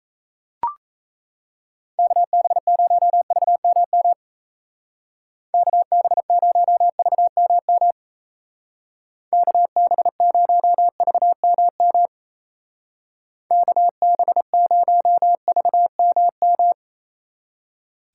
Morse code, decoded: KB0VMM KB0VMM KB0VMM KB0VMM

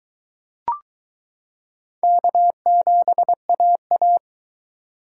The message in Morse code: K7AA

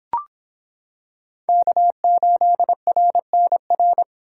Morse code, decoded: K8RNR

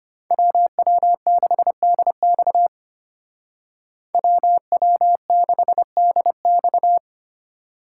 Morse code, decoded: WW6DX WW6DX

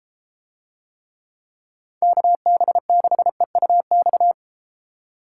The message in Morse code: KB6EUX

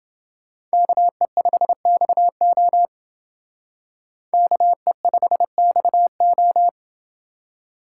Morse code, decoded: KE5XO KE5XO

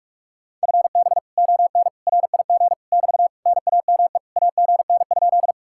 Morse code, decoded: FB ON RIG <BT> NAME AGN?

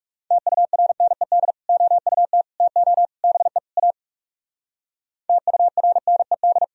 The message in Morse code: TURNED OUT TO BE A TURNED

